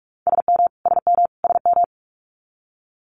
Morse code, heard as a keyed tone, about 46 wpm, 700 Hz. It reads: SM SM SM
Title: Multi-letter Phonemes Repeated 3x 45wpm